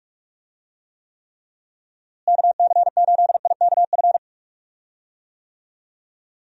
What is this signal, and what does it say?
Morse code, heard as a keyed tone, about 41 wpm, 700 Hz. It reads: KC8IKF
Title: USA Call Signs - 6 Characters - Repeated 3x - 45wpm